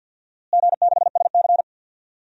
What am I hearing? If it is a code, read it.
Morse code, heard as a keyed tone, about 47 wpm, 700 Hz. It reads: G6SC
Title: DXCC-100 Call Signs - 4 Characters - Repeated 3x - 50wpm